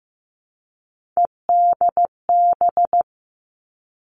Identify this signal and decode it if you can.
Morse code, heard as a keyed tone, about 15 wpm, 700 Hz. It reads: EDB
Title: Three Challenging Letter-Number Combinations 15wpm